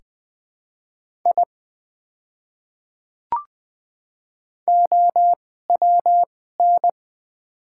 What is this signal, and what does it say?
Morse code, heard as a keyed tone, about 20 wpm, spaced as for 14 wpm, 690 Hz. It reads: I OWN
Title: 2x Character Spacing - Top 500 Words - 20wpm